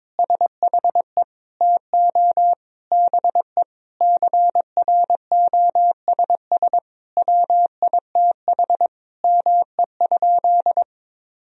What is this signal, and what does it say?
Morse code, heard as a keyed tone, about 22 wpm, 700 Hz. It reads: SHE TO BE CROSS WITH ME?
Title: Sentences from Top 1000 Words - Review Part 1 - 22wpm